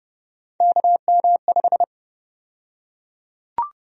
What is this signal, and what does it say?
Morse code, heard as a keyed tone, about 32 wpm, 700 Hz. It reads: KM5
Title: USA Call Sign Prefixes - Repeated 2x - 30wpm